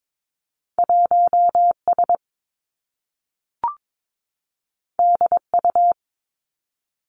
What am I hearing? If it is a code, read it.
Morse code, heard as a keyed tone, about 22 wpm, 700 Hz. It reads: 1 S D U
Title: Two Challenging Letter-Number Combinations 22wpm